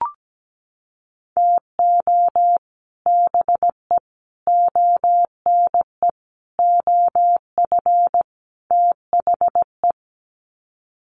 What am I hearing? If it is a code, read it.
Morse code, heard as a keyed tone, about 17 wpm, 700 Hz. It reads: TO BE ONE OF THE
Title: Sets of 5 Words 17wpm